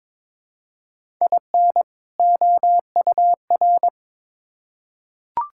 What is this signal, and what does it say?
Morse code, heard as a keyed tone, about 21 wpm, 700 Hz. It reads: IN OUR
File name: Build A Phrase - 2-Word Phrases - 22wpm